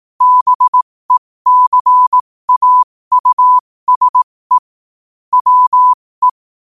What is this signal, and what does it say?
Morse code, heard as a keyed tone, about 18 wpm, spaced as for 15 wpm, 1000 Hz. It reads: BECAUSE WE
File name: qotdfiles / 15wpm